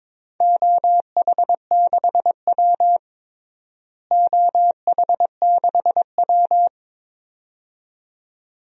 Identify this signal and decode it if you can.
Morse code, heard as a keyed tone, about 22 wpm, 700 Hz. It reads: OH6W OH6W
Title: DXCC-100 Call Signs - 4 Characters - Repeated 2x - 22wpm